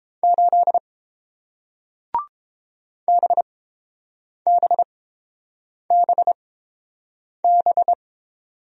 Morse code, heard as a keyed tone, about 30 wpm, spaced as for 7 wpm, 700 Hz. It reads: 8 BBBB